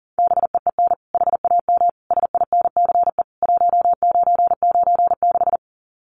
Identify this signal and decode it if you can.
Morse code, heard as a keyed tone, about 40 wpm, 700 Hz. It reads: BEEN HAM SINCE 1996
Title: Common QSO Phrases 40wpm